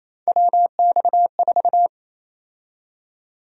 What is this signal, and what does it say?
Morse code, heard as a keyed tone, about 28 wpm, 700 Hz. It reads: WX4
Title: USA Call Sign Prefixes - 28wpm